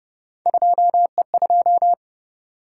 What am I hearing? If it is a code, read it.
Morse code, heard as a keyed tone, about 30 wpm, 700 Hz. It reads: 2E2